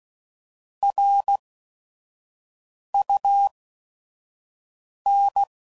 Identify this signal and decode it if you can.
Morse code, heard as a keyed tone, about 16 wpm, spaced as for 5 wpm, 800 Hz. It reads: RUN
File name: CW Code Groups @5WPM / QSOs / kc7sac